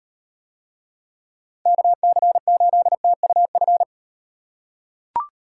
Morse code, heard as a keyed tone, about 38 wpm, 700 Hz. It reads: KC8TUF